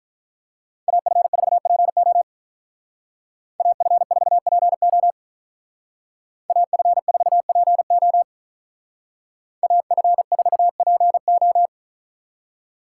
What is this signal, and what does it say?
Morse code, heard as a keyed tone, about 42 wpm, 700 Hz. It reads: AF4PO AF4PO AF4PO AF4PO
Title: Speed Racing - USA Call Signs 35wpm